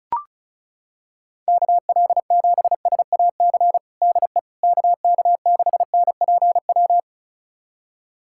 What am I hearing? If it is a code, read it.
Morse code, heard as a keyed tone, about 34 wpm, 700 Hz. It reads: KL7SAC DE KK6NPW
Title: USA Callsign DE Callsign - 6 Characters - 35wpm